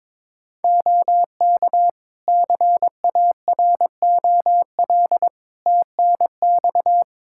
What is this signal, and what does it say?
Morse code, heard as a keyed tone, about 22 wpm, 700 Hz. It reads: OK CAROL TNX
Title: Practice QSO - Exchange 4 - Ending - 22wpm